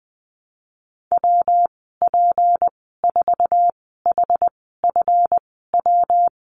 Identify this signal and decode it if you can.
Morse code, heard as a keyed tone, about 20 wpm, spaced as for 14 wpm, 700 Hz. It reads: WP4HFW
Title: 179 Lesson 29g - Slash - Call Signs 20wpm